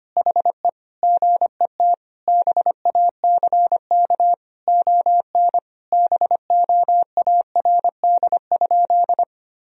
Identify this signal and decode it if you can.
Morse code, heard as a keyed tone, about 25 wpm, 700 Hz. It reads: HE GET BACK ON BOARD?
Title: Sentences from Top 1000 Words - Review Part 4 - 25wpm